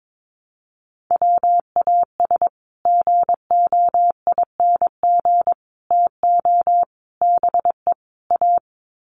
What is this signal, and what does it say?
Morse code, heard as a keyed tone, about 22 wpm, 700 Hz. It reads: WAS GOING TO BE A
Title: Sets of 5 Words 22wpm